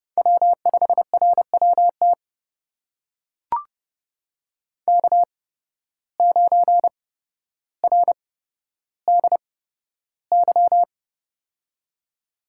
Morse code, heard as a keyed tone, about 30 wpm, spaced as for 13 wpm, 700 Hz. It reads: W5RWT K9RDY
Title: USA Call Signs - 5 Characters - 8x Character Spacing - 30wpm